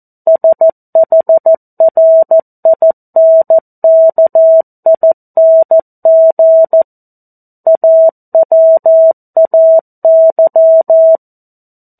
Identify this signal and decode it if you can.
Morse code, heard as a keyed tone, about 14 wpm, 650 Hz. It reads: SHRINKING AWAY
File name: 02-Chapter-02 14 WPM